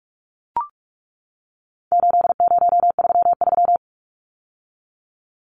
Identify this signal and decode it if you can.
Morse code, heard as a keyed tone, about 44 wpm, 700 Hz. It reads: 8033
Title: Numbers - 4-Digit 45wpm